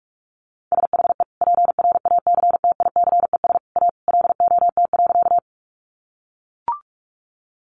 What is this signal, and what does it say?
Morse code, heard as a keyed tone, about 44 wpm, 700 Hz. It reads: SHE PRACTICES A LOT.
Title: Additional Easy Phrases 45wpm